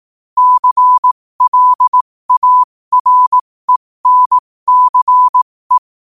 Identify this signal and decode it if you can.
Morse code, heard as a keyed tone, about 18 wpm, spaced as for 15 wpm, 1000 Hz. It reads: CLARENCE
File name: qotdfiles / 15wpm